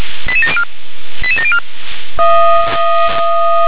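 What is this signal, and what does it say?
Five-tone selective call (ZVEI), heard as quick runs of five tones, tones of about 70 ms. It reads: 88704 88704
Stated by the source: Received on an SDR.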